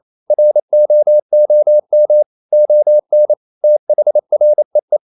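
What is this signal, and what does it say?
Morse code, heard as a keyed tone, about 28 wpm, 600 Hz. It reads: ROOM ON THREE
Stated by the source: Morse code practice at 28 wpm, text of Ken Follett's novel Never in English, from CW med Gnister.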